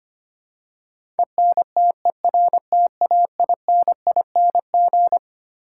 Morse code, heard as a keyed tone, about 25 wpm, 700 Hz. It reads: ENTERTAINING
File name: Twelve Letter Words - 25wpm